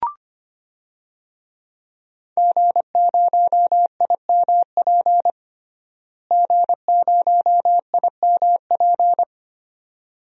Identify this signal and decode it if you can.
Morse code, heard as a keyed tone, about 25 wpm, 700 Hz. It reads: G0IMP G0IMP
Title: DXCC-100 Call Signs - 5 Characters - Repeated 2x - 25wpm